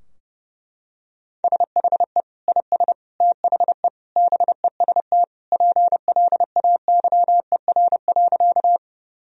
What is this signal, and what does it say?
Morse code, heard as a keyed tone, about 30 wpm, 700 Hz. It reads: SHE IS THE BEST PLAYER.
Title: Sentences from Top 1000 Words - Review Part 4 - 30wpm